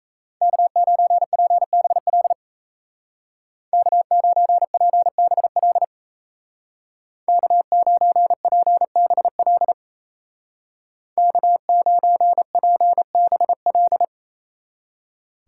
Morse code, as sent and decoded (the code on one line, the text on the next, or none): -.- ----. .--. -... .-.. / -.- ----. .--. -... .-.. / -.- ----. .--. -... .-.. / -.- ----. .--. -... .-..
K9PBL K9PBL K9PBL K9PBL